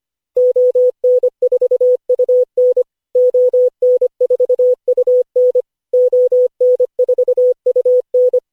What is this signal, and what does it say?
Morse code, heard as a keyed tone, about 25 wpm, 500 Hz. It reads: ON4UN ON4UN ON4UN
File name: cw / Session5 cw training